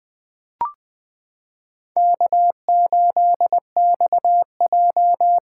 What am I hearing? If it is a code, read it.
Morse code, heard as a keyed tone, about 20 wpm, 700 Hz. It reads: K8XJ
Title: K1USN SST QSO Practice - 20wpm